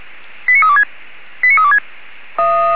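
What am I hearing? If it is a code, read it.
Five-tone selective call (ZVEI), heard as quick runs of five tones, tones of about 70 ms. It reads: 89326 89326
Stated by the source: Five-tone ZVEI selective call received by SDR, from the feuerwehr-orlamuende.org fire brigade server.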